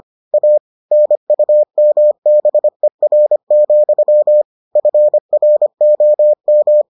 Morse code, heard as a keyed tone, about 25 wpm, 600 Hz. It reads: A NUMBER, FROM